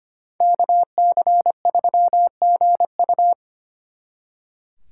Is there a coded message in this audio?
Yes: Morse code from 0.4 to 3.3 s, heard as a keyed tone, about 33 wpm, 700 Hz. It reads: KC3GU